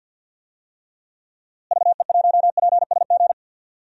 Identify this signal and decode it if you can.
Morse code, heard as a keyed tone, about 52 wpm, 700 Hz. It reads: VE1PSG
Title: DXCC-100 Call Signs - 6 Characters - Repeated 3x - 50wpm